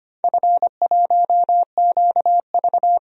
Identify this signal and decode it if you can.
Morse code, heard as a keyed tone, about 25 wpm, 700 Hz. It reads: F1QV
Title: DXCC-100 Call Signs - 4 Characters - Repeated 3x - 25wpm